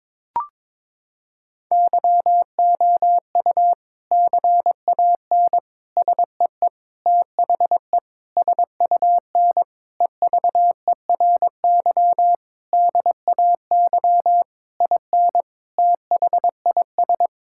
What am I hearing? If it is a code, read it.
Morse code, heard as a keyed tone, about 22 wpm, 700 Hz. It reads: YOU CAN SEE THE SUN EVERY DAY IN THIS
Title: Sentences from Top 1600 Words 22wpm